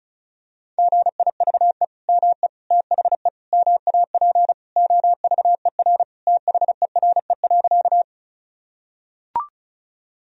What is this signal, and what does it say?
Morse code, heard as a keyed tone, about 36 wpm, 700 Hz. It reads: GIVE ME THE MAP OVER THERE.
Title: Sentences from Top 1000 Words - Review Part 2 - 35wpm